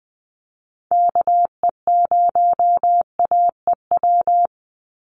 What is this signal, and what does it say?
Morse code, heard as a keyed tone, about 20 wpm, 700 Hz. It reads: KE0AEW
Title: USA Call Signs 20wpm